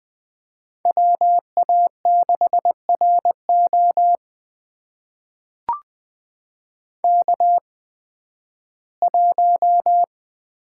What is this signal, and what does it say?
Morse code, heard as a keyed tone, about 20 wpm, spaced as for 13 wpm, 700 Hz. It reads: WA6RO K1
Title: USA Call Signs - 8x Character Spacing - 20wpm